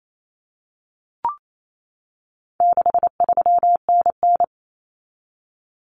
Morse code, heard as a keyed tone, about 28 wpm, 700 Hz. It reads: 63NN
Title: Common Cut Numbers - 4-Digit 28wpm